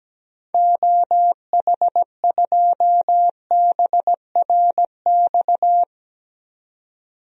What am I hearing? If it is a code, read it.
Morse code, heard as a keyed tone, about 17 wpm, 700 Hz. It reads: OH2BRX
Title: DXCC-100 Call Signs - 6 Characters - 17wpm